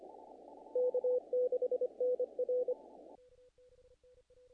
Morse code, heard as a keyed tone, about 28 wpm, 500 Hz. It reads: K6NR
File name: contests / naqp-cw-k5tr-op-k5pi- -1800